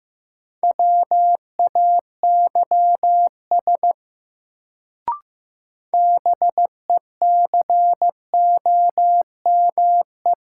Morse code, heard as a keyed tone, about 15 wpm, 700 Hz. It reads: WAYS BECOME
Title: Sets of 5 Words Encore - New Word Drill - 15wpm